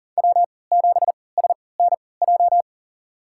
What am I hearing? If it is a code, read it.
Morse code, heard as a keyed tone, about 37 wpm, spaced as for 22 wpm, 700 Hz. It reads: W7SNJ
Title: USA Call Signs - 5 Characters - 3x Character Spacing - 40wpm